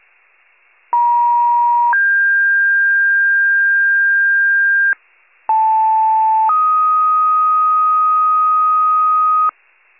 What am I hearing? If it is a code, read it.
Two-tone page, heard as a one-second tone, then a three-second tone, about 950 then 1600 Hz, about 880 then 1250 Hz, with noise. It is heard from 0.9 to 9.5 s.
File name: wcalert